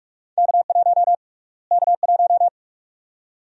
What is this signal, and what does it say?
Morse code, heard as a keyed tone, about 46 wpm, 690 Hz. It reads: K1 K1
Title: USA Call Sign Prefixes - Repeated 2x - 45wpm